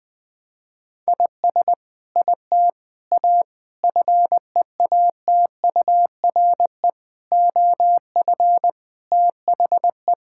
Morse code, heard as a keyed tone, about 20 wpm, 700 Hz. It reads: IS IT A FEATURE OF THE
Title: Sentences from Top 1000 Words - Review Part 5 - 20wpm